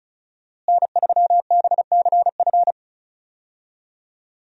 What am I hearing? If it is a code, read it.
Morse code, heard as a keyed tone, about 36 wpm, 700 Hz. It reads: N3BCF
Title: USA Call Signs - 5 Characters - Repeated 3x - 35wpm